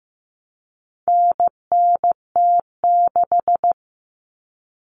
Morse code, heard as a keyed tone, about 15 wpm, 700 Hz. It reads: NNT6
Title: Common Cut Numbers - 4-Digit 15wpm